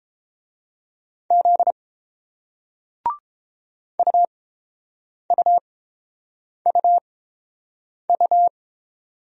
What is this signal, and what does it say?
Morse code, heard as a keyed tone, about 29 wpm, spaced as for 7 wpm, 700 Hz. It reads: Z UUUU